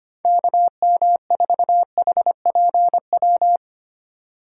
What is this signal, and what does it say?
Morse code, heard as a keyed tone, about 25 wpm, 700 Hz. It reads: KM4HPW
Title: USA Call Signs - 6 Characters - 2x Character Spacing - 25wpm